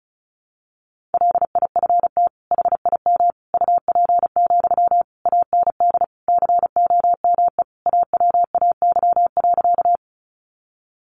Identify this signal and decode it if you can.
Morse code, heard as a keyed tone, about 34 wpm, 700 Hz. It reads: LIFT HIM UP, AND COME AWAY.
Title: Sentences from Top 1000 Words 35wpm